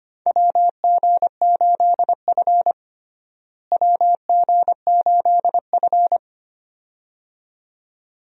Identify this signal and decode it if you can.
Morse code, heard as a keyed tone, about 25 wpm, 700 Hz. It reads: WG8F WG8F
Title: USA Call Signs - 4 Characters - Repeated 2x - 25wpm